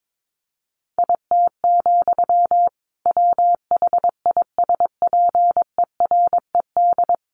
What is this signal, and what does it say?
Morse code, heard as a keyed tone, about 22 wpm, 700 Hz. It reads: IT, WHISPERED
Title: A Sound Of Thunder 22wpm